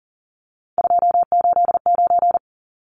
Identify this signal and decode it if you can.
Morse code, heard as a keyed tone, about 40 wpm, 700 Hz. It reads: 289
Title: Numbers - 3-Digit 40wpm